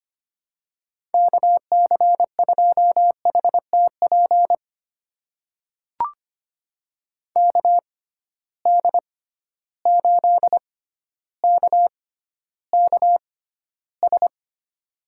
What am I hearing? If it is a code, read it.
Morse code, heard as a keyed tone, about 25 wpm, spaced as for 13 wpm, 700 Hz. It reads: KC2HTP KD8KKS
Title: USA Call Signs - 6 Characters - 6x Character Spacing - 25wpm